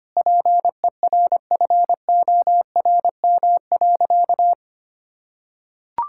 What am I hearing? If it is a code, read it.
Morse code, heard as a keyed tone, about 25 wpm, 700 Hz. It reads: PERFORM.
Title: Sentences from Top 1000 Words - Review Part 5 - 25wpm